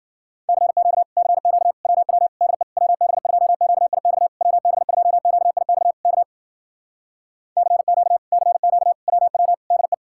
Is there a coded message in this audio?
Yes: Morse code from 0.5 to 9.9 s, heard as a keyed tone, about 60 wpm, 700 Hz. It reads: CQ CQ WW DE WB2QIX WB2QIX K CQ CQ WW DE